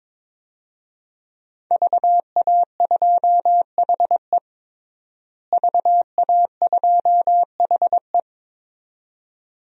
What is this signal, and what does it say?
Morse code, heard as a keyed tone, about 21 wpm, 700 Hz. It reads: VA2HE VA2HE
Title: DXCC-100 Call Signs - 5 Characters - Repeated 2x - 22wpm